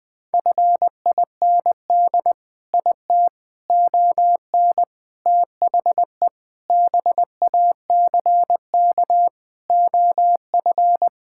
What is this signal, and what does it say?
Morse code, heard as a keyed tone, about 20 wpm, 700 Hz. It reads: FIND IT ON THE BACK OF
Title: Sentences with 4-Word Phrases - 20wpm